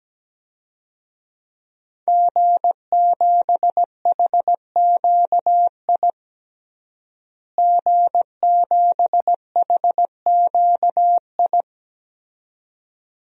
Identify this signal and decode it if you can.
Morse code, heard as a keyed tone, about 17 wpm, 700 Hz. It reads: G7HQI G7HQI
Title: DXCC-100 Call Signs - 5 Characters - Repeated 2x - 17wpm